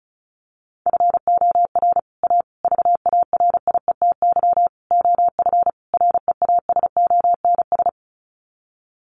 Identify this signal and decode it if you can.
Morse code, heard as a keyed tone, about 34 wpm, 700 Hz. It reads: FOR A VARIETY OF REASONS
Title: Sets of 5 Words Encore 35wpm